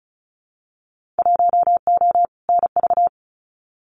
Morse code, heard as a keyed tone, about 36 wpm, 700 Hz. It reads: 1O NV
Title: Winter Field Day Exchanges Encore 35wpm